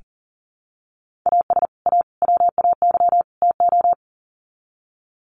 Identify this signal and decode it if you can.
Morse code, heard as a keyed tone, about 40 wpm, 700 Hz. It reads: AS A WAY TO
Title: Sets of 4 Words 40wpm